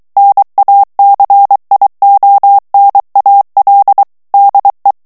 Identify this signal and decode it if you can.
Morse code, heard as a keyed tone, about 23 wpm, 800 Hz. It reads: NACIONAL DE
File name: morse mp3 / ESPAGNOL TEXTE EN CLAIR 2 VITESSE 22 MOTS MINUTES